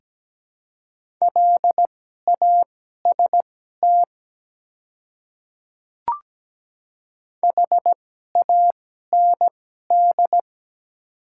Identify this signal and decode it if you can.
Morse code, heard as a keyed tone, about 17 wpm, spaced as for 12 wpm, 700 Hz. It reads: LAST HAND
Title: 2x Character Spacing - Top 300 Words - 17wpm